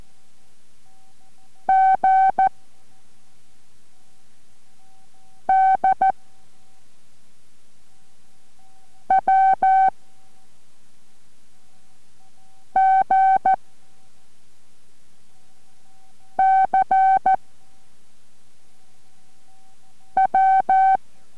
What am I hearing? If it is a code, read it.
Morse code, heard as a keyed tone, about 14 wpm, 770 Hz. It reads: G D W G C W